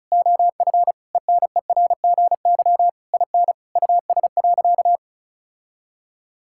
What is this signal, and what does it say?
Morse code, heard as a keyed tone, about 36 wpm, 700 Hz. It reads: OF ENERGY IN US.
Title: Sentences from Top 1000 Words - Review Part 6 - 35wpm